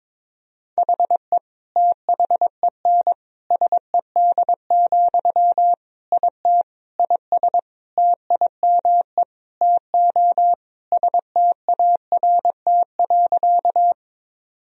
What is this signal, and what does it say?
Morse code, heard as a keyed tone, about 22 wpm, 700 Hz. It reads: HE THEN SED, IT IS TIME TO START.
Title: Sentences from Top 300 Words and Abbreviations 22wpm